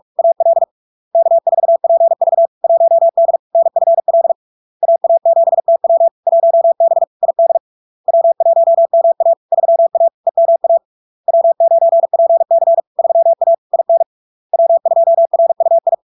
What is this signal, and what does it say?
Morse code, heard as a keyed tone, about 46 wpm, 670 Hz. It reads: AL K4PV 1D NFL AA7TW 1B ID W1MA 3A EMA W9PC 3A IN W2RUI
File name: LARK-FD-2022-Log-in-CW-45wpm-670-Hz